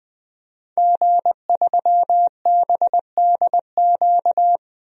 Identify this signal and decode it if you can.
Morse code, heard as a keyed tone, about 20 wpm, 700 Hz. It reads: G3BDQ
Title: DXCC-100 Call Signs - 5 Characters - 20wpm